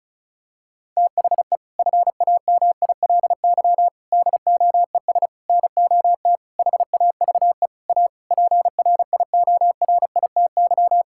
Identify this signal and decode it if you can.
Morse code, heard as a keyed tone, about 36 wpm, 700 Hz. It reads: THE FAMILY DOES NOT HAVE A PRIORITY
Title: Sentences from Top 1600 Words 35wpm